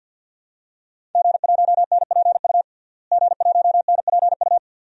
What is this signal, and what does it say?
Morse code, heard as a keyed tone, about 46 wpm, 700 Hz. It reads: G1NPU G1NPU
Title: DXCC-100 Call Signs - 5 Characters - Repeated 2x - 50wpm